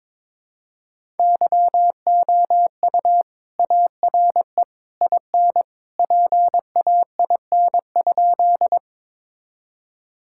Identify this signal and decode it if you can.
Morse code, heard as a keyed tone, about 22 wpm, 700 Hz. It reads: YOU ARE IN PAIN?
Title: Sentences from Top 1600 Words 22wpm